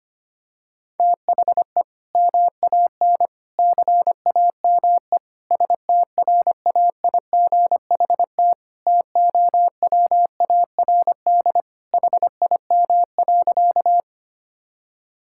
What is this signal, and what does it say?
Morse code, heard as a keyed tone, about 25 wpm, 700 Hz. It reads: THE MAN CAME STRAIGHT TOWARD HIM.
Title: Sentences from Top 1000 Words - Review Part 6 - 25wpm